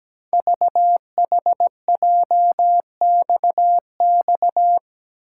Morse code, heard as a keyed tone, about 17 wpm, 700 Hz. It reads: VHJXX
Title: Code Groups - 5 Letters - 17wpm